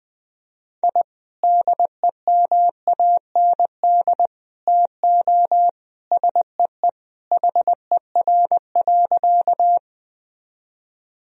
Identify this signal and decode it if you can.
Morse code, heard as a keyed tone, about 20 wpm, 700 Hz. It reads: I DEMAND TO SEE HER.
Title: Sentences from Top 1000 Words - Review Part 4 - 20wpm